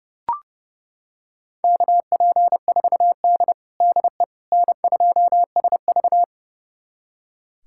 Morse code, heard as a keyed tone, about 30 wpm, 700 Hz. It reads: KP4D DE N2SV
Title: USA Callsign DE Callsign - 4 Characters - 30wpm